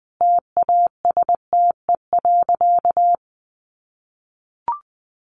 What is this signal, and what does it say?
Morse code, heard as a keyed tone, about 20 wpm, 700 Hz. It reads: TASTE.
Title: Sentences from Top 1100 Words 20wpm